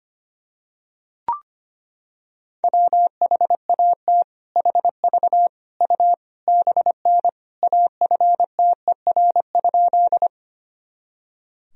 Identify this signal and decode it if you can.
Morse code, heard as a keyed tone, about 25 wpm, 700 Hz. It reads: WHAT HV U BN AFTER?
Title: Sentences from Top 200 Words and Abbreviations 25wpm